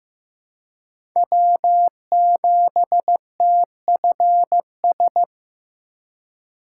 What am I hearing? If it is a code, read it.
Morse code, heard as a keyed tone, about 16 wpm, 700 Hz. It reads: W7TFS
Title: USA Call Signs - 5 Characters - 4x Character Spacing - 15wpm